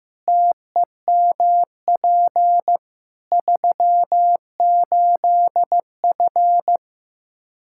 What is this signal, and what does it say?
Morse code, heard as a keyed tone, about 15 wpm, 700 Hz. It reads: TEMP 38F
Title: Practice QSO - Exchange 2 - Info - 15wpm